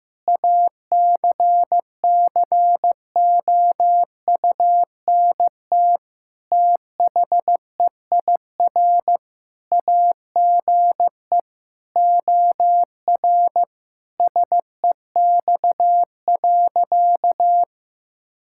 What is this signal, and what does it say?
Morse code, heard as a keyed tone, about 16 wpm, 700 Hz. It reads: ACCOUNT THEIR AGE OR SEX.
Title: Sentences from Top 1600 Words 15wpm